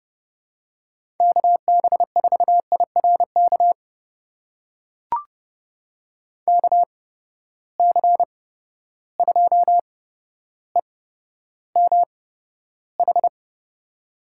Morse code, heard as a keyed tone, about 32 wpm, spaced as for 13 wpm, 700 Hz. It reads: KB4IRK KC2EMH
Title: USA Call Signs - 6 Characters - 8x Character Spacing - 30wpm